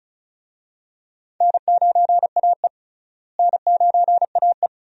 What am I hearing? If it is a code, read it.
Morse code, heard as a keyed tone, about 36 wpm, 700 Hz. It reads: N9AE N9AE
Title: USA Call Signs - 4 Characters - Repeated 2x - 35wpm